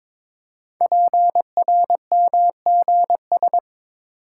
Morse code, heard as a keyed tone, about 21 wpm, 700 Hz. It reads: PRMGS
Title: Code Groups - 5 Letters - 22wpm